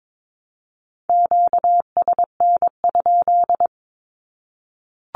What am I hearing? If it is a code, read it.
Morse code, heard as a keyed tone, about 22 wpm, 700 Hz. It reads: QSN?
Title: Q-Codes 22wpm